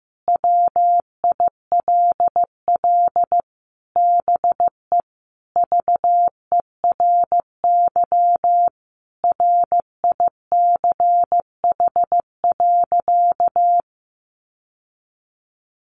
Morse code, heard as a keyed tone, about 15 wpm, 700 Hz. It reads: WILL BE VERY RICH.
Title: Sentences from Top 1200 Words 15wpm